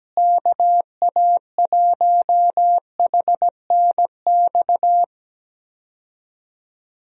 Morse code, heard as a keyed tone, about 17 wpm, 700 Hz. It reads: KA1HNX
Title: USA Call Signs - 6 Characters - Repeated 2x - 17wpm